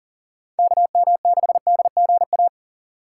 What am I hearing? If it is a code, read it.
Morse code, heard as a keyed tone, about 40 wpm, 700 Hz. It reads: KM6DGA
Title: USA Call Signs - 6 Characters - 4x Character Spacing - 40wpm